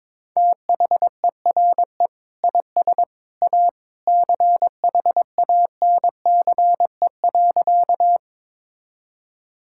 Morse code, heard as a keyed tone, about 22 wpm, 700 Hz. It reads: THERE IS A CHANCE.
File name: practice / 22wpm / Sentences with 2-Word Phrases - 22wpm